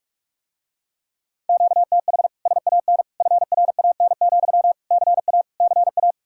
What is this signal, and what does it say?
Morse code, heard as a keyed tone, about 49 wpm, 700 Hz. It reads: QTH SAN FRAN, CA CA